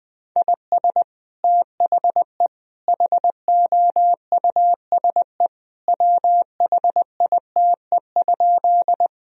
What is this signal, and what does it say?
Morse code, heard as a keyed tone, about 21 wpm, 700 Hz. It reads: IS THE HOUSE WHITE?
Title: Sentences from Top 1000 Words - Review Part 1 - 20wpm